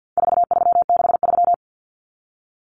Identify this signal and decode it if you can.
Morse code, heard as a keyed tone, about 48 wpm, 700 Hz. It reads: 4363
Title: Common Cut Numbers - 4-Digit 50wpm